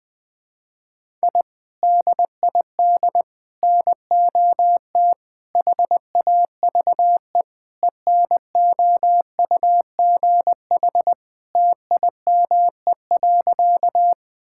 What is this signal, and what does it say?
Morse code, heard as a keyed tone, about 21 wpm, 700 Hz. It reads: I DID NOT HAVE ENOUGH TIME.
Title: Sentences with 4-Word Phrases - Spoken Beginning - 20wpm